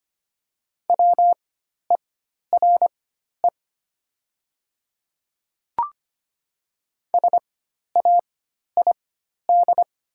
Morse code, heard as a keyed tone, about 25 wpm, spaced as for 12 wpm, 700 Hz. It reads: WERE SAID
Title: 4x Character Spacing - Top 200 Words - 25wpm